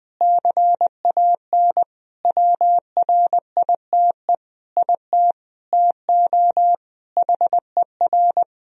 Morse code, heard as a keyed tone, about 20 wpm, 700 Hz. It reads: CAN WRITE IT TO HER
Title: Sentences from Top 1000 Words - Review Part 4 - 20wpm